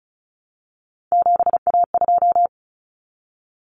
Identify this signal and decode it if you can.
Morse code, heard as a keyed tone, about 34 wpm, 700 Hz. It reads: 7A2
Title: Common Cut Numbers - 3-Digit 35wpm